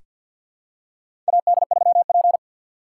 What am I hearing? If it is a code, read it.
Morse code, heard as a keyed tone, about 48 wpm, 700 Hz. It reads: AD3P
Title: USA Call Signs - 4 Characters - 2x Character Spacing - 50wpm